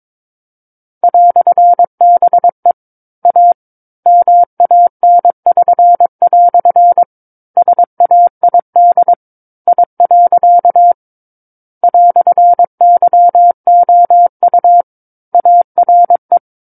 Morse code, heard as a keyed tone, about 22 wpm, 700 Hz. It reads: "BE A MAN<SN>" SAID I. "YOU ARE
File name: Book One / 13-Chapter 13 - 22 WPM